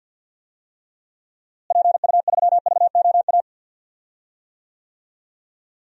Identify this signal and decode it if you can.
Morse code, heard as a keyed tone, about 52 wpm, 700 Hz. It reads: PU3VOA
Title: DXCC-100 Call Signs - 6 Characters - Repeated 3x - 50wpm